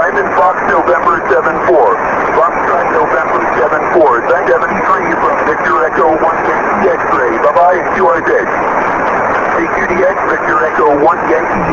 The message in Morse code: none